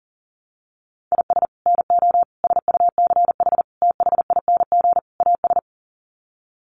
Morse code, heard as a keyed tone, about 40 wpm, 700 Hz. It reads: IS NO SUCH THING AS
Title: Sets of 5 Words Encore 40wpm